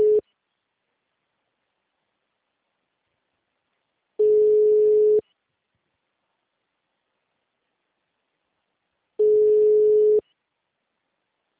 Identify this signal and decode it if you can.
Ringback tone (Europe, 425 Hz).
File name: out- -2022- - - .32